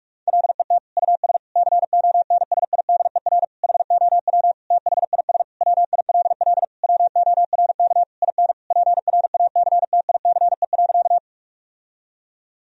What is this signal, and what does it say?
Morse code, heard as a keyed tone, about 41 wpm, 700 Hz. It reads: LET US CONSIDER HOW THIS WILL WORK IN PRACTICE.
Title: Sentences from Top 1000 Words - Review Part 4 - 45wpm